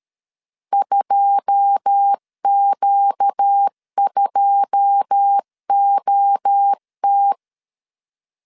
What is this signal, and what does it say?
Morse code, heard as a keyed tone, about 13 wpm, 800 Hz. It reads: 2Q2OT
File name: LEZIONI / 9 2 12wpm 2301